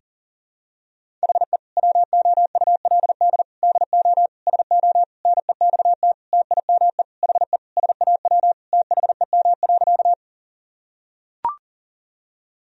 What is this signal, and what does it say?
Morse code, heard as a keyed tone, about 37 wpm, 700 Hz. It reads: HE WOULD DO SO NEXT TIME HE SAW THEM.
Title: Sentences from Top 1000 Words - Review Part 1 - 40wpm